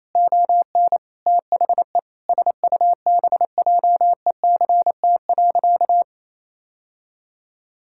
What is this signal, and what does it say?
Morse code, heard as a keyed tone, about 28 wpm, 700 Hz. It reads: ON THE SUBJECT.
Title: Sentences from Top 1000 Words - Review Part 6 - 28wpm